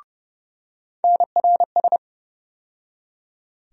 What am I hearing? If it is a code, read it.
Morse code, heard as a keyed tone, about 32 wpm, 700 Hz. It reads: NRS
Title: USA Call Sign Postfixes - 3 Letters 30wpm